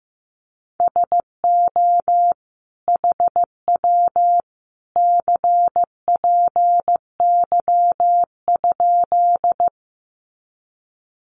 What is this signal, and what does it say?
Morse code, heard as a keyed tone, about 15 wpm, 700 Hz. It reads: SO HW CPY?
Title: Common QSO Phrases 15wpm